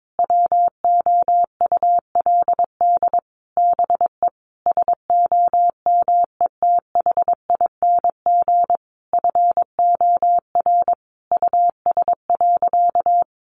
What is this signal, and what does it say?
Morse code, heard as a keyed tone, about 22 wpm, 700 Hz. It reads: WOULD BE SOMETHING FOR US.
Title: Sentences from Top 1200 Words 22wpm